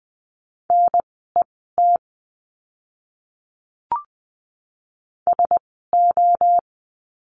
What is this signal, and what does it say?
Morse code, heard as a keyed tone, about 20 wpm, spaced as for 14 wpm, 700 Hz. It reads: NET SO
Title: 036 Lesson 7f - Letter s - Words 20wpm